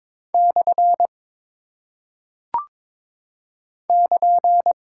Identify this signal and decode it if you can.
Morse code, heard as a keyed tone, about 22 wpm, 700 Hz. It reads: / <KN>
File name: practice / 22wpm / Single Punctuation-Prosign - 22wpm